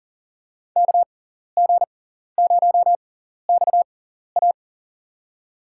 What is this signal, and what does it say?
Morse code, heard as a keyed tone, about 39 wpm, spaced as for 14 wpm, 700 Hz. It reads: KG0XA
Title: USA Call Signs - 5 Characters - 6x Character Spacing - 40wpm